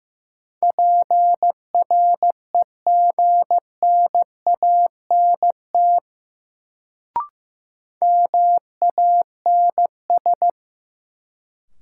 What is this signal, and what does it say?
Morse code, heard as a keyed tone, about 15 wpm, 700 Hz. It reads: PREGNANT MANS
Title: Sentences from Top 1000 Words - Review Part 6 - New Word Drill - 15wpm